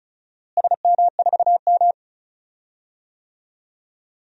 Morse code, heard as a keyed tone, about 34 wpm, 700 Hz. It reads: SM4M
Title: DXCC-100 Call Signs - 4 Characters - Repeated 3x - 35wpm